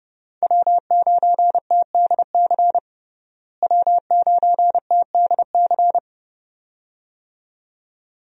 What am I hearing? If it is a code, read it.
Morse code, heard as a keyed tone, about 30 wpm, 700 Hz. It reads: W9TDC W9TDC
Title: USA Call Signs - 5 Characters - Repeated 2x - 30wpm